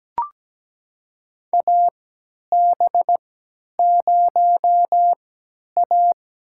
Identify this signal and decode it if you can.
Morse code, heard as a keyed tone, about 17 wpm, spaced as for 10 wpm, 700 Hz. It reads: AB0A
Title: USA Call Signs - 3x Character Spacing - 17wpm